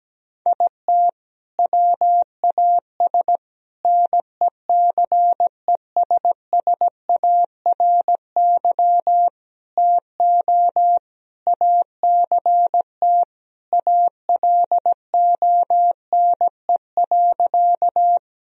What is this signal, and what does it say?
Morse code, heard as a keyed tone, about 17 wpm, 700 Hz. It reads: IT WAS NECESSARY TO ACT ALONE.
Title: Sentences from Top 1000 Words - Review Part 1 - 17wpm